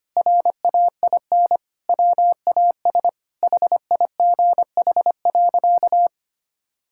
Morse code, heard as a keyed tone, about 25 wpm, 700 Hz. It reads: RAIN WAS HIGH.
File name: practice / 25wpm / Sentences with 3-Word Phrases - 25wpm